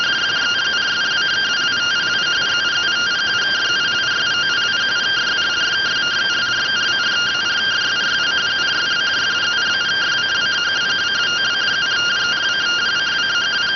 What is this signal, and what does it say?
Two-tone teleprinter signal, about 170 Hz shift.